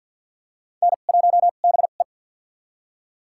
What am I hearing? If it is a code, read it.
Morse code, heard as a keyed tone, about 47 wpm, spaced as for 35 wpm, 700 Hz. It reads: N1BE
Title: USA Call Signs - 4 Characters - 2x Character Spacing - 50wpm